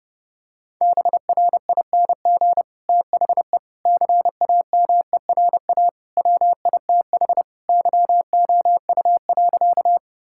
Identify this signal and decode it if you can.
Morse code, heard as a keyed tone, about 32 wpm, 700 Hz. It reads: BRING THE CAMERA WITH YOU.
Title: Sentences from Top 1000 Words - Review Part 5 - 30wpm